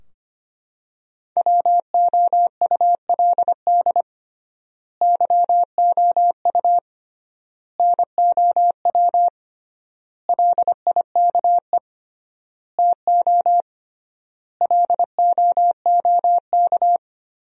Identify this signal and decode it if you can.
Morse code, heard as a keyed tone, about 25 wpm, 700 Hz. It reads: WOULD YOU NOW LIKE TO LOOK
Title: 3x Word-Spacing - Sentences from Top 200 Words - 25wpm